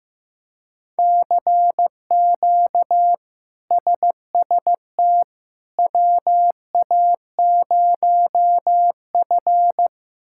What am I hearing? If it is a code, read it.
Morse code, heard as a keyed tone, about 15 wpm, 700 Hz. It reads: CQ SST WA0F
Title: K1USN SST QSO Practice - 15wpm